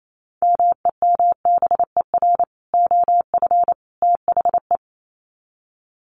Morse code, heard as a keyed tone, about 28 wpm, 700 Hz. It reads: MEMBER OF THE
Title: Sets of 3 Words 28wpm